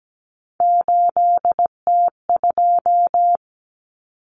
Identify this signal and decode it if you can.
Morse code, heard as a keyed tone, about 17 wpm, 700 Hz. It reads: 8T2
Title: Common Cut Numbers - 3-Digit 17wpm